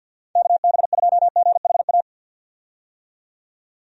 Morse code, heard as a keyed tone, about 51 wpm, 700 Hz. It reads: KB2GHA
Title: USA Call Signs - 6 Characters - 50wpm